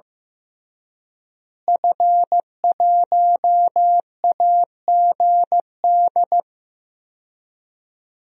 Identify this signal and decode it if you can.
Morse code, heard as a keyed tone, about 15 wpm, 700 Hz. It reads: F1AGD
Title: DXCC-100 Call Signs - 5 Characters - Repeated 3x - 15wpm